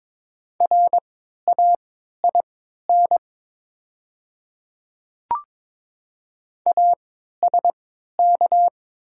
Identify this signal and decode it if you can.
Morse code, heard as a keyed tone, about 22 wpm, spaced as for 12 wpm, 700 Hz. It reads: RAIN ASK
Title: 3x Character Spacing - Top 300 Words - 22wpm